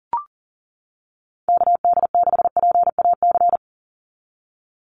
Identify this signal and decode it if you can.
Morse code, heard as a keyed tone, about 40 wpm, 700 Hz. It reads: KD6PAC
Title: USA Call Signs 40wpm